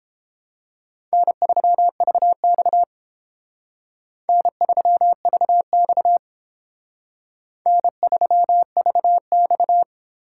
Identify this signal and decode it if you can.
Morse code, heard as a keyed tone, about 31 wpm, 700 Hz. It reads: N3VX N3VX N3VX